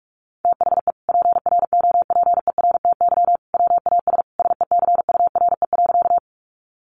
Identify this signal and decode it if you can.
Morse code, heard as a keyed tone, about 44 wpm, 700 Hz. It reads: THE PROPERTY WAS SECURE.
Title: Sentences from Top 1200 Words 45wpm